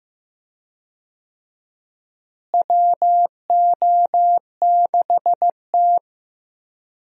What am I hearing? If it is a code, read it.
Morse code, heard as a keyed tone, about 15 wpm, 700 Hz. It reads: WO6T
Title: USA Call Signs - 4 Characters - Repeated 3x - 15wpm